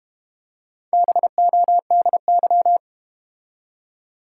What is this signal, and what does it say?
Morse code, heard as a keyed tone, about 32 wpm, 700 Hz. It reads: BODY